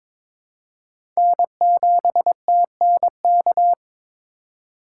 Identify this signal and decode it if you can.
Morse code, heard as a keyed tone, about 22 wpm, 700 Hz. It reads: N7TNK
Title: USA Call Signs - 2x Character Spacing - 22wpm